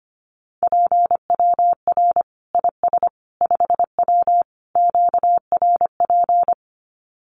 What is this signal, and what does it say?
Morse code, heard as a keyed tone, about 25 wpm, 700 Hz. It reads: PWR IS 5W QRP
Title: Common QSO Phrases 25wpm